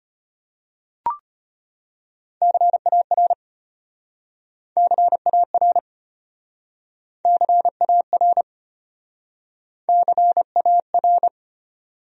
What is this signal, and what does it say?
Morse code, heard as a keyed tone, about 33 wpm, 700 Hz. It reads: CAR CAR CAR CAR